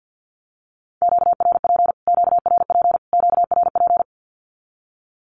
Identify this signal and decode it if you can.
Morse code, heard as a keyed tone, about 48 wpm, 700 Hz. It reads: QRP QRP QRP
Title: Q-Codes Repeated 3x 50wpm